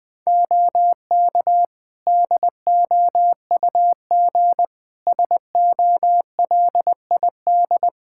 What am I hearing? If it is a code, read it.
Morse code, heard as a keyed tone, about 21 wpm, 700 Hz. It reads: OK DOUG SOLID
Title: K1USN SST QSO Practice 20wpm